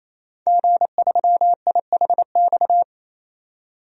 Morse code, heard as a keyed tone, about 28 wpm, 700 Hz. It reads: G3IHX
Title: DXCC-100 Call Signs - 5 Characters - 28wpm